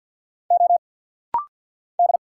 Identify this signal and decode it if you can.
Morse code, heard as a keyed tone, about 50 wpm, 690 Hz. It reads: O D
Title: Single Letter-Number Alt - Rapid-Fire 50wpm